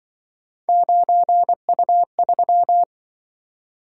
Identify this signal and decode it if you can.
Morse code, heard as a keyed tone, about 25 wpm, 700 Hz. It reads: 9U3
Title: Speed-Racing - Three Letter-Number Combinations - 20wpm